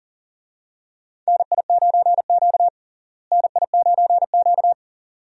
Morse code, heard as a keyed tone, about 37 wpm, 700 Hz. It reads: NI9Q NI9Q
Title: USA Call Signs - 4 Characters - Repeated 2x - 40wpm